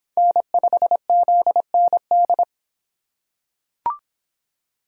Morse code, heard as a keyed tone, about 23 wpm, 700 Hz. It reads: N5ZND